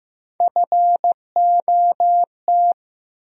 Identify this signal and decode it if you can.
Morse code, heard as a keyed tone, about 15 wpm, 700 Hz. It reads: FOT
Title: USA Call Sign Postfixes - 3 Letters 15wpm